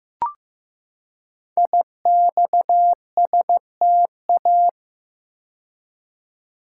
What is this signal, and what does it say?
Morse code, heard as a keyed tone, about 15 wpm, 700 Hz. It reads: IXSTA
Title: Code Groups - 5 Letters - 15wpm